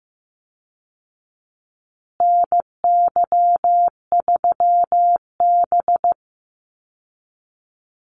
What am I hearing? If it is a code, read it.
Morse code, heard as a keyed tone, about 15 wpm, 700 Hz. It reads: NY3B
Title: Call Signs Repeated 3x - 15wpm